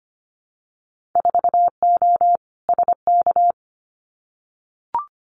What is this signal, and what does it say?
Morse code, heard as a keyed tone, about 25 wpm, 690 Hz. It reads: 4O SK
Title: Winter Field Day Exchanges Encore 25wpm